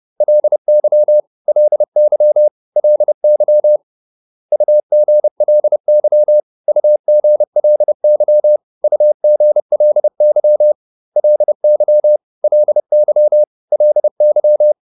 Morse code, heard as a keyed tone, about 30 wpm, 600 Hz. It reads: LY LY LY UGLY UGLY UGLY LY LY LY